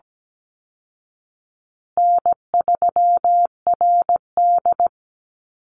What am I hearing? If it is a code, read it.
Morse code, heard as a keyed tone, about 17 wpm, 700 Hz. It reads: N3RD
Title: Easy Call Signs Repeated 2x - 17wpm